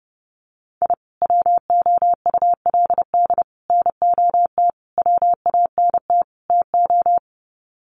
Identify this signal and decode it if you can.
Morse code, heard as a keyed tone, about 30 wpm, 700 Hz. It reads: I WOULD NOT WANT TO
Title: Sets of 5 Words 30wpm